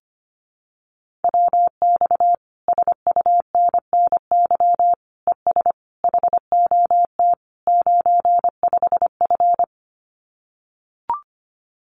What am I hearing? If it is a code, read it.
Morse code, heard as a keyed tone, about 25 wpm, 700 Hz. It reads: WX SUNNY ES HOT 95F
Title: Common QSO Phrases Encore 25wpm